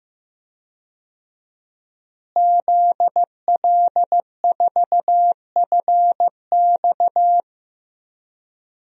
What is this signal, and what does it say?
Morse code, heard as a keyed tone, about 16 wpm, 700 Hz. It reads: ZL4FX
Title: DXCC-100 Call Signs - 5 Characters - Repeated 3x - 15wpm